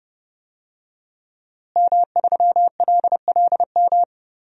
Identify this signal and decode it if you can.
Morse code, heard as a keyed tone, about 32 wpm, 700 Hz. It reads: M3LLM
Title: DXCC-100 Call Signs - 5 Characters - Repeated 2x - 30wpm